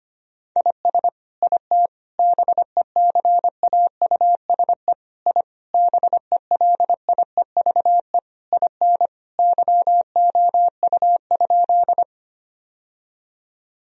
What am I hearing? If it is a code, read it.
Morse code, heard as a keyed tone, about 25 wpm, 700 Hz. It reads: IS IT BECAUSE I BELIEVE IN YOU?
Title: Sentences from Top 1000 Words - Review Part 3 - 25wpm